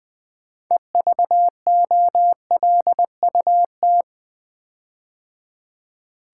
Morse code, heard as a keyed tone, about 20 wpm, 700 Hz. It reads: EVOLUT
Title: Build A Word - 6 Letters At A Time - 12-Letter Words - 20wpm